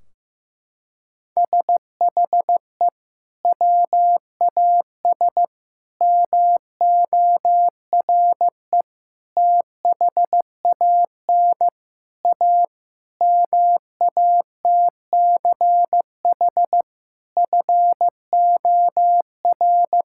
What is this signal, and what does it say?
Morse code, heard as a keyed tone, about 15 wpm, 700 Hz. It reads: SHE WAS MORE THAN A MATCH FOR
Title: Sentences from Top 1600 Words 15wpm